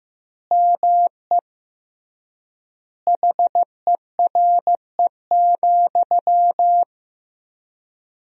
Morse code, heard as a keyed tone, about 15 wpm, 700 Hz. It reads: ME HERE,
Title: 3x Word-Spacing - Sentences from Top 500 Words - 15wpm